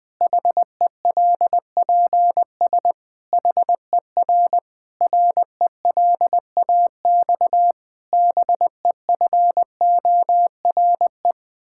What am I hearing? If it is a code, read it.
Morse code, heard as a keyed tone, about 20 wpm, 700 Hz. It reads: HELPS HER RELAX BEFORE
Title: Difficult Sentences with Hints - 20wpm